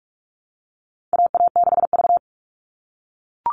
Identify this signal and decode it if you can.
Morse code, heard as a keyed tone, about 44 wpm, 700 Hz. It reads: AA6V
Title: All Cut Numbers - 4-Digit 45wpm